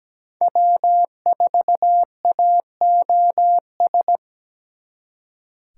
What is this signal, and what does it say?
Morse code, heard as a keyed tone, about 20 wpm, 700 Hz. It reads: W4AOS
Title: Speed Racing - USA 1x3 Call Signs 17wpm